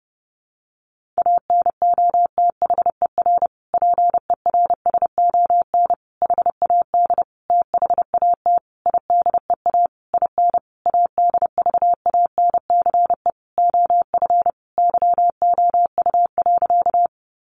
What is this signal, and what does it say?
Morse code, heard as a keyed tone, about 30 wpm, 700 Hz. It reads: ANOTHER PERSON HAD THAT IDEA IN ADVANCE OF YOU.
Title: Sentences from Top 1400 Words 30wpm